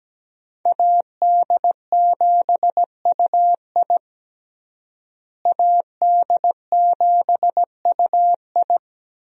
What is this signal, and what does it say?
Morse code, heard as a keyed tone, about 17 wpm, 700 Hz. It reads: AD7UI AD7UI
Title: USA Call Signs - 5 Characters - Repeated 3x - 17wpm